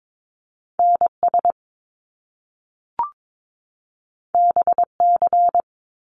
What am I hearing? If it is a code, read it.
Morse code, heard as a keyed tone, about 22 wpm, 700 Hz. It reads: NS BC
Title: Canadian Provinces and Territories 22wpm